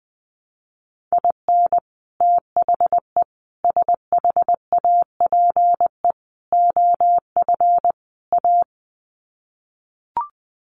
Morse code, heard as a keyed tone, about 19 wpm, 700 Hz. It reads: IN THE SHAPE OF A
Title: Sets of 5 Words Encore 20wpm